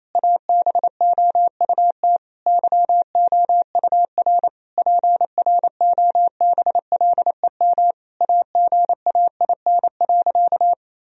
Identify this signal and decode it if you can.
Morse code, heard as a keyed tone, about 28 wpm, 700 Hz. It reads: ABOUT YOUR PROBLEM AGAIN.
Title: Sentences from Top 1000 Words - Review Part 3 - 28wpm